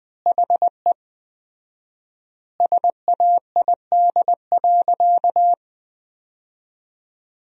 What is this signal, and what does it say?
Morse code, heard as a keyed tone, about 20 wpm, 700 Hz. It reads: HE SAID.
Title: 4x Word-Spacing - Sentences from Top 200 Words - 20wpm